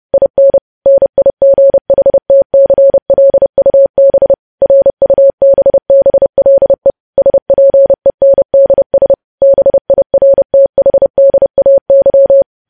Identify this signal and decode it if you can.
Morse code, heard as a keyed tone, about 30 wpm, 550 Hz. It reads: IN NIGHTCLUB RUBBLE SPENDS BIRTHDAY
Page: news headlines in morse code at 30 wpm